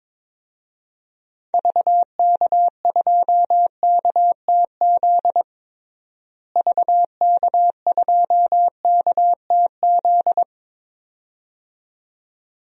Morse code, heard as a keyed tone, about 21 wpm, 700 Hz. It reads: VK2KTZ VK2KTZ
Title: DXCC-100 Call Signs - 6 Characters - Repeated 2x - 22wpm